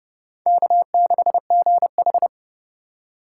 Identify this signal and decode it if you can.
Morse code, heard as a keyed tone, about 30 wpm, 700 Hz. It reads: K6GH
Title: USA Call Signs - 4 Characters - 4x Character Spacing - 30wpm